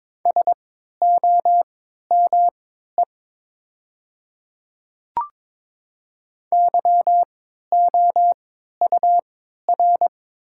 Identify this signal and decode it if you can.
Morse code, heard as a keyed tone, about 22 wpm, spaced as for 13 wpm, 690 Hz. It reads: SOME YOUR
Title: 3x Character Spacing - Top 100 Words - 22wpm